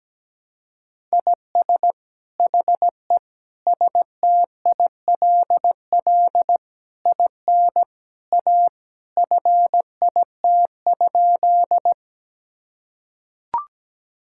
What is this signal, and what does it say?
Morse code, heard as a keyed tone, about 17 wpm, 700 Hz. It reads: IS HE STILL IN A FIT?
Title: Sentences from Top 1000 Words - Review Part 5 - 17wpm